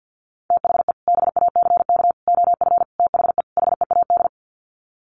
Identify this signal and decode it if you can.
Morse code, heard as a keyed tone, about 48 wpm, 700 Hz. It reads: THE BACK OF THE HEAD
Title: Sets of 5 Words Encore2 50wpm